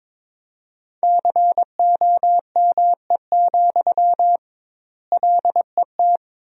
Morse code, heard as a keyed tone, about 22 wpm, 700 Hz. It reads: COME, LET
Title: 2x Word-Spacing - Sentences from Top 500 Words - 22wpm